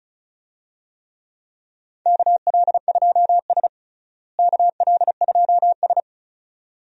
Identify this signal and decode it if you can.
Morse code, heard as a keyed tone, about 34 wpm, 700 Hz. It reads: KL2S KL2S
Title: USA Call Signs - 4 Characters - Repeated 2x - 35wpm